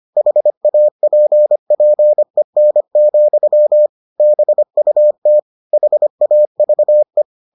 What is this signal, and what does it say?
Morse code, heard as a keyed tone, about 25 wpm, 600 Hz. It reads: HAPPEN, BUT HAVE